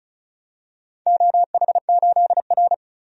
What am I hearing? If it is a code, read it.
Morse code, heard as a keyed tone, about 36 wpm, 700 Hz. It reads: OH8R
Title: DXCC-100 Call Signs - 4 Characters - 35wpm